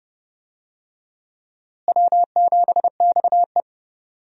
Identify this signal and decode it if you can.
Morse code, heard as a keyed tone, about 30 wpm, 700 Hz. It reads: W7XE
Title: USA Call Signs - 4 Characters - Repeated 3x - 30wpm